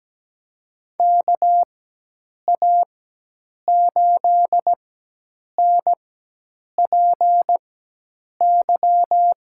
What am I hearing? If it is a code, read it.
Morse code, heard as a keyed tone, about 17 wpm, spaced as for 8 wpm, 700 Hz. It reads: KA8NPY
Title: USA Call Signs - 4x Character Spacing - 17wpm